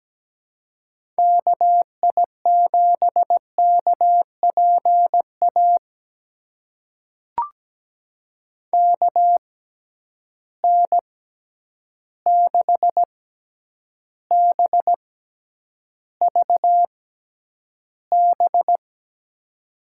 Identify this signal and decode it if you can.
Morse code, heard as a keyed tone, about 17 wpm, spaced as for 9 wpm, 700 Hz. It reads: KI7KPA KN6BVB
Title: USA Call Signs - 6 Characters - 6x Character Spacing - 17wpm